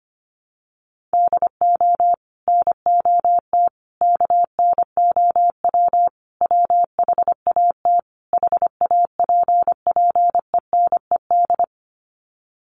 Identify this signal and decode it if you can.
Morse code, heard as a keyed tone, about 25 wpm, 700 Hz. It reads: DO NOT KNOW WHAT HAPPENED
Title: Sets of 5 Words Encore2 25wpm